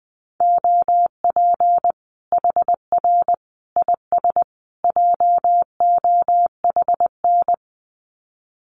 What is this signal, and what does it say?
Morse code, heard as a keyed tone, about 20 wpm, 700 Hz. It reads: OP HR IS JOHN
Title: Common QSO Phrases 20wpm